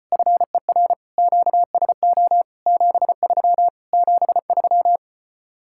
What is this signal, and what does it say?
Morse code, heard as a keyed tone, about 33 wpm, 700 Hz. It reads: FER QSO 73 73